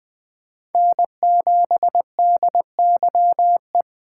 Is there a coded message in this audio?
Yes: Morse code from 0.8 to 3.8 s, heard as a keyed tone, about 21 wpm, 700 Hz. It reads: N7DYE